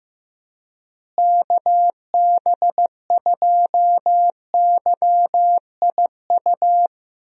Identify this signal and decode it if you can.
Morse code, heard as a keyed tone, about 15 wpm, 700 Hz. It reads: KB2YIU
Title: USA Call Signs - 6 Characters - Repeated 3x - 15wpm